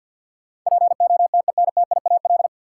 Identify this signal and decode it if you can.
Morse code, heard as a keyed tone, about 52 wpm, 700 Hz. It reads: POTENTIAL
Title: Sentences from Top 1000 Words - Review Part 4 - New Word Drill - 50wpm